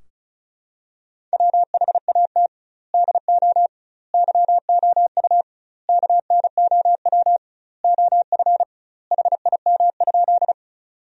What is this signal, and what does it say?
Morse code, heard as a keyed tone, about 36 wpm, 700 Hz. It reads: WHAT DO YOU KNOW OF HIM?
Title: 2x Word-Spacing - Sentences from Top 100 Words - 35wpm